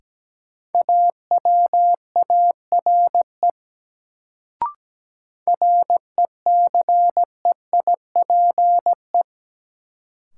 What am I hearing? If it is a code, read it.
Morse code, heard as a keyed tone, about 17 wpm, 700 Hz. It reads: AWARE RECEIPE
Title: Sentences from Top 1200 Words - New Word Drill - 17wpm